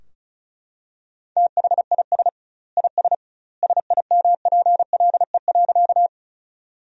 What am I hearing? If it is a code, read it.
Morse code, heard as a keyed tone, about 34 wpm, 700 Hz. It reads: THIS IS SIMPLE.
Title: 2x Word-Spacing - Sentences from Top 400 Words - 35wpm